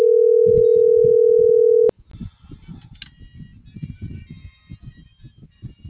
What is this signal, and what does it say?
Ringback tone (North America), 440 + 480 Hz.